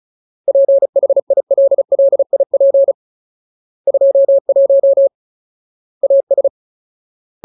Morse code, heard as a keyed tone, about 34 wpm, 550 Hz. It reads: PHILLIP 21 AS
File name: Ultimate Word List / 35WPM Ultimate